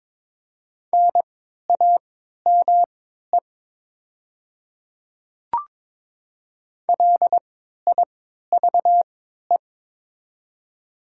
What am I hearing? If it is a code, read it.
Morse code, heard as a keyed tone, about 22 wpm, spaced as for 12 wpm, 700 Hz. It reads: NAME LIVE
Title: 3x Character Spacing - Top 200 Words - 22wpm